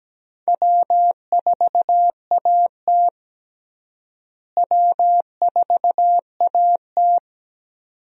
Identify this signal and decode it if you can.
Morse code, heard as a keyed tone, about 17 wpm, 700 Hz. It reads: W4AT W4AT
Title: USA Call Signs - 4 Characters - Repeated 2x - 17wpm